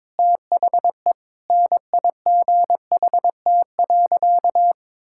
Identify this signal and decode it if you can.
Morse code, heard as a keyed tone, about 22 wpm, 700 Hz. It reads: THE NIGHT.
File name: Sentences with 7-Word Phrases - 22wpm